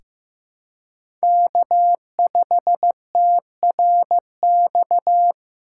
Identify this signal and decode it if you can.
Morse code, heard as a keyed tone, about 15 wpm, 700 Hz. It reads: K5TRX
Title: USA Call Signs - 3x Character Spacing - 15wpm